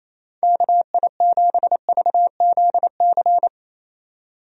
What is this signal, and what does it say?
Morse code, heard as a keyed tone, about 27 wpm, 700 Hz. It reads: KI7VZC
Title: USA Call Signs - 6 Characters - 28wpm